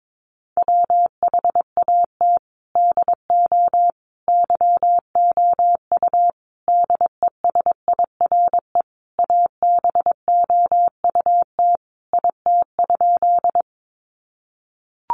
Morse code, heard as a keyed tone, about 22 wpm, 700 Hz. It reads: WHAT DO YOU DESIRE ABOUT IT?
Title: Sentences from Top 1100 Words 22wpm